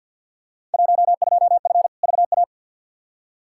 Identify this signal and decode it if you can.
Morse code, heard as a keyed tone, about 53 wpm, 700 Hz. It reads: 12F VA
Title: Field Day Exchanges Encore - 50wpm